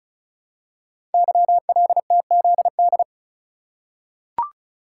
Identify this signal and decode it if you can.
Morse code, heard as a keyed tone, about 36 wpm, 700 Hz. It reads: YLTZD
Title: Code Groups - 5 Letter:Numbers - 35wpm